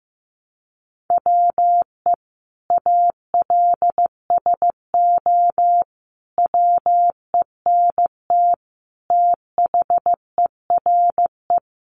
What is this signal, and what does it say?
Morse code, heard as a keyed tone, about 15 wpm, 700 Hz. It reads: WE ALSO WENT THERE
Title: Sentences from Top 1200 Words 15wpm